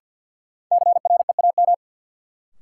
Morse code, heard as a keyed tone, about 52 wpm, 700 Hz. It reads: CREAM
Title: Sentences from Top 1500 Words - New Word Drill Alt - 50wpm